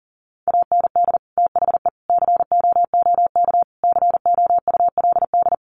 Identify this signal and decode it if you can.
Morse code, heard as a keyed tone, about 40 wpm, 700 Hz. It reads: AND THE COOK COULD
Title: Sentences from Top 1300 Words 40wpm